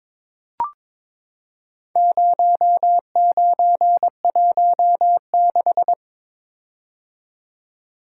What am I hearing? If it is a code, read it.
Morse code, heard as a keyed tone, about 22 wpm, 700 Hz. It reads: 0916
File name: practice / 22wpm / Four Letter-Number Combinations - 22wpm